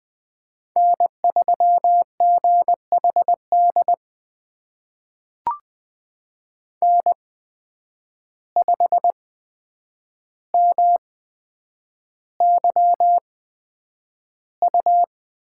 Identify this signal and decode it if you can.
Morse code, heard as a keyed tone, about 20 wpm, spaced as for 9 wpm, 700 Hz. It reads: N3GHD N5MYU
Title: USA Call Signs - 5 Characters - 8x Character Spacing - 20wpm